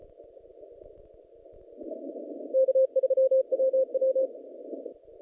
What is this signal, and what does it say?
Morse code, heard as a keyed tone, about 33 wpm, 530 Hz. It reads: K3WW
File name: contests / naqp-cw-k5tr-op-k5pi- -0300